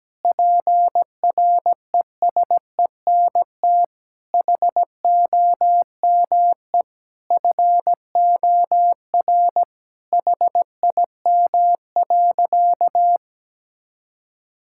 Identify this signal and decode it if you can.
Morse code, heard as a keyed tone, about 17 wpm, 700 Hz. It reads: PRESENT HOME FOR HIM.
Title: Sentences from Top 1000 Words - Review Part 5 - 17wpm